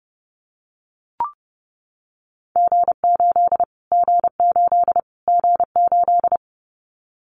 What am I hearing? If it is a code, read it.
Morse code, heard as a keyed tone, about 30 wpm, 700 Hz. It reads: G8 G8 G8
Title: Callsign Prefixes for Top 25 DXCC - Repeated 3x - 30wpm